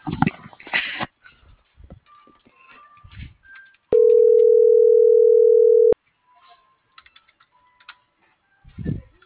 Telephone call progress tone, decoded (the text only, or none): ringback tone